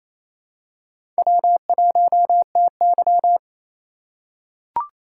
Morse code, heard as a keyed tone, about 28 wpm, 700 Hz. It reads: W1TY
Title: USA Call Signs - 2x Character Spacing - 28wpm